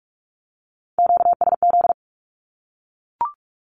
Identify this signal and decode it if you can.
Morse code, heard as a keyed tone, about 44 wpm, 700 Hz. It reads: QSZ
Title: Q-Codes Repeated 3x 45wpm